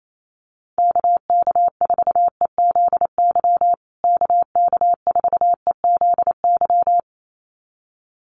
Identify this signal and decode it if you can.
Morse code, heard as a keyed tone, about 28 wpm, 700 Hz. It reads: KK4EZY KK4EZY
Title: USA Call Signs Repeated 2x 28wpm